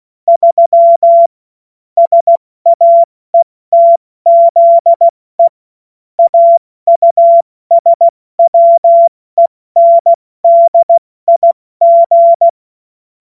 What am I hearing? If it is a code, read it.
Morse code, heard as a keyed tone, about 16 wpm, 680 Hz. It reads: 3 SAETZE AUSWENDIG
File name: Morsekurs / Witze / Witz 24. August 2021, 70 F80